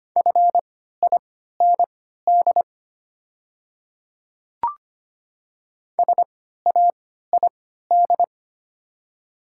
Morse code, heard as a keyed tone, about 25 wpm, spaced as for 14 wpm, 700 Hz. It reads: FIND SAID
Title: 3x Character Spacing - Top 100 Words - 25wpm